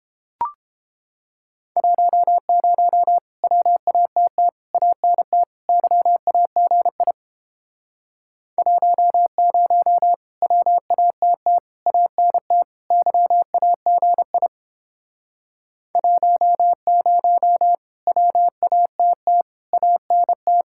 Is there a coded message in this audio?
Yes: Morse code from 1.8 to 20.6 s, heard as a keyed tone, about 29 wpm, 700 Hz. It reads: 10 WATT ANT YAGI 10 WATT ANT YAGI 10 WATT ANT